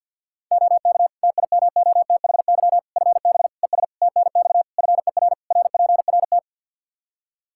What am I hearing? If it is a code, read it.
Morse code, heard as a keyed tone, about 52 wpm, 700 Hz. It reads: OK TIMOTHY FB ES TNX FER RPRT
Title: Practice QSO - Exchange 2 - Info - 50wpm